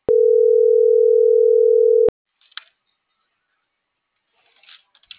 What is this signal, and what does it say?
Ringback tone (North America), 440 + 480 Hz.